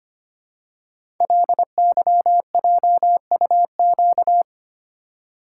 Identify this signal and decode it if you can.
Morse code, heard as a keyed tone, about 25 wpm, 700 Hz. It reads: LYJUQ
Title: Code Groups - 5 Letters - 25wpm